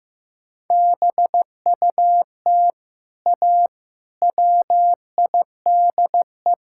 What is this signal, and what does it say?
Morse code, heard as a keyed tone, about 15 wpm, 700 Hz. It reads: BUT A WIDE
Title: Sentences from Top 1000 Words - Review Part 6 - 15wpm